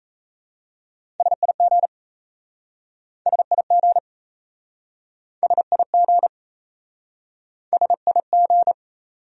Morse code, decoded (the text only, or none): SIG SIG SIG SIG